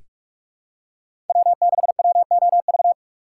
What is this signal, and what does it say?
Morse code, heard as a keyed tone, about 43 wpm, 700 Hz. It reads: W6WOV
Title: USA Call Signs - 5 Characters - 2x Character Spacing - 45wpm